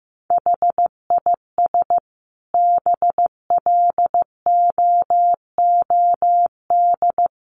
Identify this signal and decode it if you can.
Morse code, heard as a keyed tone, about 15 wpm, 700 Hz. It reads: HIS BLOOD
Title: Sentences from Top 1100 Words 15wpm